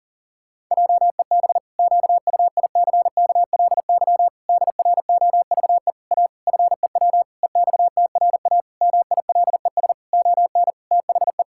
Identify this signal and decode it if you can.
Morse code, heard as a keyed tone, about 40 wpm, 700 Hz. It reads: JEB QUICKLY DROVE A FEW EXTRA MILES ON THE
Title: Pangram Challenge v2 - 40wpm